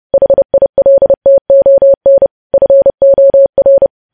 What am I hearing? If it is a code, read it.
Morse code, heard as a keyed tone, about 30 wpm, 550 Hz. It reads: HILTON FOR